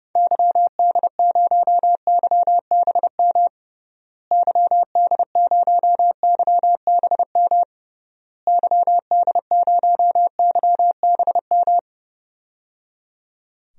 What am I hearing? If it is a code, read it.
Morse code, heard as a keyed tone, about 30 wpm, 700 Hz. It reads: YD0YBM YD0YBM YD0YBM
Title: DXCC-100 Call Signs - 6 Characters - Repeated 3x - 30wpm